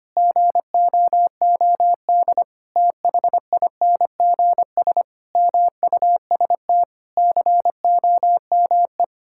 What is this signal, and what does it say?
Morse code, heard as a keyed tone, about 25 wpm, 700 Hz. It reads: GOOD THINGS MUST COME
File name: practice / 25wpm / Sentences with 9-Word Phrases - 25wpm